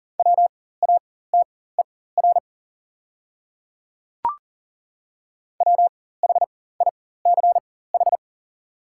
Morse code, heard as a keyed tone, about 40 wpm, spaced as for 19 wpm, 690 Hz. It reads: WATER WHICH
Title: 4x Character Spacing - Top 300 Words - 40wpm